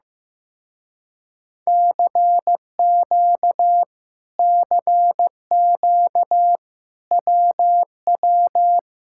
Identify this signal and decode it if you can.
Morse code, heard as a keyed tone, about 15 wpm, 700 Hz. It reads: CQ CQ WW
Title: Speed-Racing Common QSO Phrases Encore - 15wpm